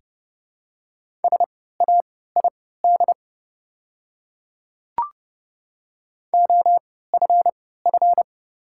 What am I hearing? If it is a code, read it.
Morse code, heard as a keyed tone, about 30 wpm, spaced as for 17 wpm, 700 Hz. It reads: SAID OFF
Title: 3x Character Spacing - Top 200 Words - 30wpm